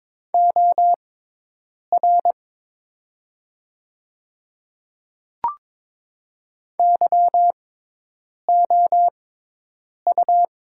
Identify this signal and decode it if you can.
Morse code, heard as a keyed tone, about 22 wpm, spaced as for 8 wpm, 700 Hz. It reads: OR YOU
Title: 6x Character Spacing - Top 200 Words - 22wpm